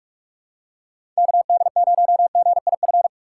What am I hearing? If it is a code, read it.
Morse code, heard as a keyed tone, about 46 wpm, 700 Hz. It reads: KD0GIF
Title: USA Call Signs - 6 Characters - 4x Character Spacing - 45wpm